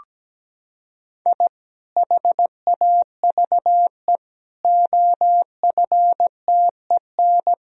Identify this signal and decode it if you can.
Morse code, heard as a keyed tone, about 17 wpm, 700 Hz. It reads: I HAVE OFTEN
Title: Sentences from Top 1000 Words - Review Part 1 - 17wpm